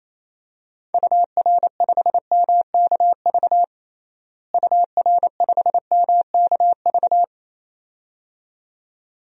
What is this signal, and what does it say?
Morse code, heard as a keyed tone, about 28 wpm, 700 Hz. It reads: UR5MKV UR5MKV
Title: DXCC-100 Call Signs - 6 Characters - Repeated 2x - 28wpm